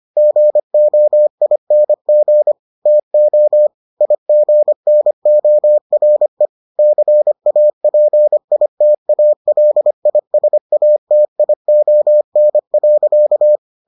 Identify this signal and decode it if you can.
Morse code, heard as a keyed tone, about 25 wpm, 600 Hz. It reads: GOING TO IGNORE CAPITALISATION.